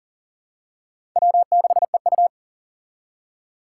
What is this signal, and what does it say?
Morse code, heard as a keyed tone, about 37 wpm, 700 Hz. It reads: W6EU
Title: K1USN SST QSO Practice - 40wpm